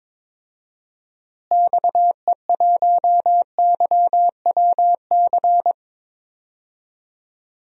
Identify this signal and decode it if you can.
Morse code, heard as a keyed tone, about 22 wpm, 700 Hz. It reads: XE1YWC
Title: DXCC-100 Call Signs - 6 Characters - Repeated 3x - 22wpm